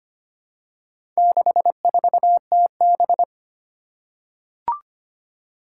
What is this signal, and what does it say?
Morse code, heard as a keyed tone, about 25 wpm, 700 Hz. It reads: 64TB
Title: Four Challenging Letter-Number Combinations 25wpm